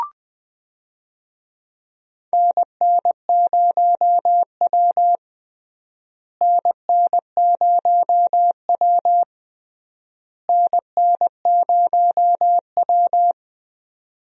Morse code, heard as a keyed tone, about 20 wpm, 700 Hz. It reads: NN0W NN0W NN0W
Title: USA Call Signs - 4 Characters - Repeated 3x - 20wpm